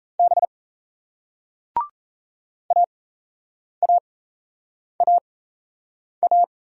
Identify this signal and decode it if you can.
Morse code, heard as a keyed tone, about 38 wpm, spaced as for 8 wpm, 700 Hz. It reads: B AAAA